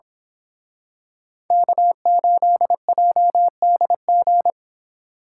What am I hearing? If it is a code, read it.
Morse code, heard as a keyed tone, about 27 wpm, 700 Hz. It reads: K8JDG
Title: Speed Racing - USA 1x3 Call Signs 22wpm